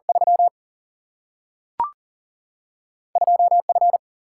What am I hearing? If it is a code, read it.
Morse code, heard as a keyed tone, about 40 wpm, 700 Hz. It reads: 3 2 F
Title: Single-Letter Single-Number 40wpm